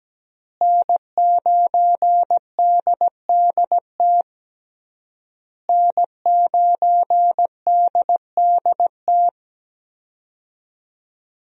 Morse code, heard as a keyed tone, about 17 wpm, 700 Hz. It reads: N9DDT N9DDT
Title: USA Call Signs - 5 Characters - Repeated 3x - 17wpm